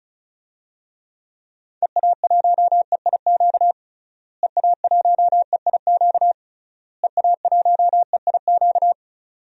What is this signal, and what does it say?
Morse code, heard as a keyed tone, about 34 wpm, 700 Hz. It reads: EA1EIQ EA1EIQ EA1EIQ
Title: DXCC-100 Call Signs - 6 Characters - Repeated 3x - 35wpm